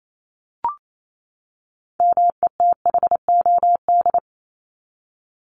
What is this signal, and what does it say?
Morse code, heard as a keyed tone, about 28 wpm, 700 Hz. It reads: METHOD
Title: Top 501-750 Words 28wpm